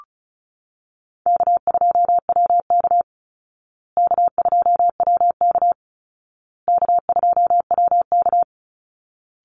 Morse code, heard as a keyed tone, about 34 wpm, 700 Hz. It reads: K2WK K2WK K2WK
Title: Easy Call Signs Repeated 3x - 35wpm